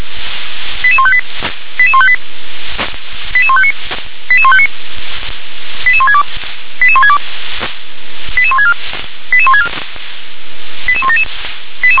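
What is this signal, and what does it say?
Five-tone selective call (ZVEI), heard as quick runs of five tones, tones of about 70 ms. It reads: 88158 88158 88159 88159 88162 88162 88164 88164 88177
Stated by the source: Five-tone ZVEI selective call received on SDR.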